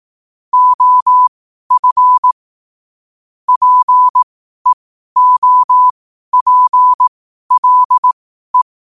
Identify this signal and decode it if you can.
Morse code, heard as a keyed tone, about 18 wpm, spaced as for 12 wpm, 1000 Hz. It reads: OF PEOPLE
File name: qotdfiles / 12wpm